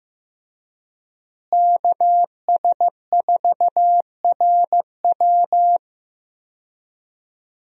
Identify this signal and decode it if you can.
Morse code, heard as a keyed tone, about 15 wpm, 700 Hz. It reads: KS4RW
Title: USA Call Signs - 5 Characters - Repeated 3x - 15wpm